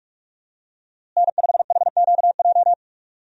Morse code, heard as a keyed tone, about 46 wpm, 700 Hz. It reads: N5HQJ
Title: USA Call Signs - 5 Characters - Repeated 3x - 45wpm